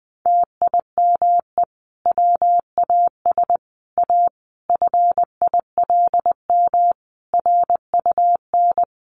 Morse code, heard as a keyed tone, about 19 wpm, 700 Hz. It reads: TIME WAS A FILM RUN
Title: A Sound Of Thunder 20wpm